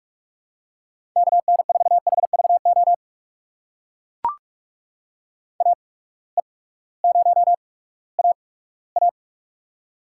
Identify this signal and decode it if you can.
Morse code, heard as a keyed tone, about 44 wpm, spaced as for 20 wpm, 700 Hz. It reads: KN4HVO AE0AA
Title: USA Call Signs - 8x Character Spacing - 45wpm